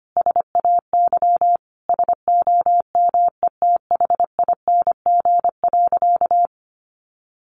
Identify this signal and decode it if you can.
Morse code, heard as a keyed tone, about 25 wpm, 700 Hz. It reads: SAY SOMETHING.
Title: Sentences from Top 600 Words 25wpm